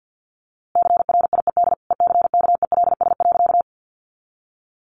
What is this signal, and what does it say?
Morse code, heard as a keyed tone, about 50 wpm, 700 Hz. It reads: CRIED ECKELS.
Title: A Sound Of Thunder 50wpm